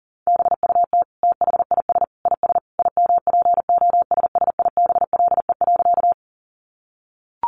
Morse code, heard as a keyed tone, about 40 wpm, 700 Hz. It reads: BUT THIS IS IMPOSSIBLE.
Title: Sentences from Top 1100 Words 40wpm